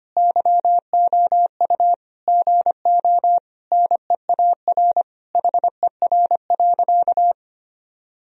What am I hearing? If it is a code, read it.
Morse code, heard as a keyed tone, about 25 wpm, 700 Hz. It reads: YOU GO NEAR HER.
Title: Sentences from Top 1000 Words - Review Part 4 - 25wpm